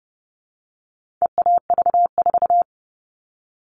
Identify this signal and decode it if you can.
Morse code, heard as a keyed tone, about 30 wpm, 700 Hz. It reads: EAV4
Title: All Cut Numbers - 4-Digit 30wpm